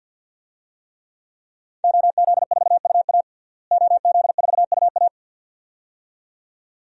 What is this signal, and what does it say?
Morse code, heard as a keyed tone, about 52 wpm, 700 Hz. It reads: OZ4UA OZ4UA
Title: DXCC-100 Call Signs - 5 Characters - Repeated 2x - 50wpm